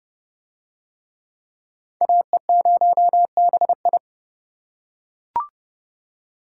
Morse code, heard as a keyed tone, about 28 wpm, 700 Hz. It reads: AE0BI